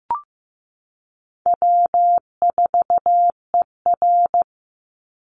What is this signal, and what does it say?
Morse code, heard as a keyed tone, about 15 wpm, 700 Hz. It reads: W4ER
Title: Call Signs Repeated 3x - 15wpm